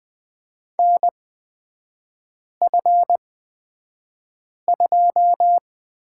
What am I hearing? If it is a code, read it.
Morse code, heard as a keyed tone, about 20 wpm, spaced as for 5 wpm, 700 Hz. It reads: NF2
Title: 250 Lesson 40e - BK - Three Characters 20wpm